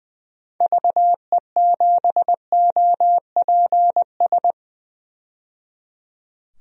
Morse code, heard as a keyed tone, about 20 wpm, 700 Hz. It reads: VE7OPS
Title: DXCC-100 Call Signs - 6 Characters - 20wpm